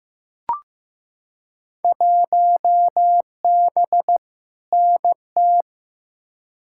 Morse code, heard as a keyed tone, about 15 wpm, 700 Hz. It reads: 1B NT
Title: Field Day Exchanges Encore - 15wpm